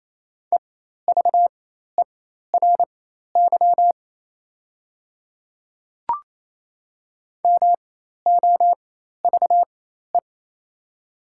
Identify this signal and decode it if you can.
Morse code, heard as a keyed tone, about 27 wpm, spaced as for 13 wpm, 690 Hz. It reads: EVERY MOVE
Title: 4x Character Spacing - Top 300 Words - 28wpm